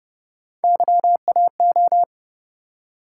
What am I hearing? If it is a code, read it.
Morse code, heard as a keyed tone, about 31 wpm, 700 Hz. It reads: YAO